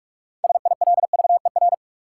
Morse code, heard as a keyed tone, about 44 wpm, 700 Hz. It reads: SILVER
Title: Six Letter Words - 45wpm